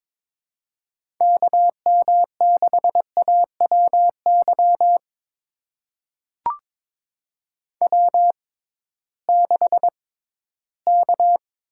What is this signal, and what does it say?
Morse code, heard as a keyed tone, about 22 wpm, spaced as for 14 wpm, 700 Hz. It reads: KM6AWY W6K
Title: USA Call Signs - 6x Character Spacing - 22wpm